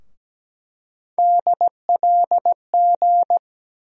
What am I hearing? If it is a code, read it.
Morse code, heard as a keyed tone, about 17 wpm, 700 Hz. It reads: DLG
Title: USA Call Sign Postfixes - 3 Letters 17wpm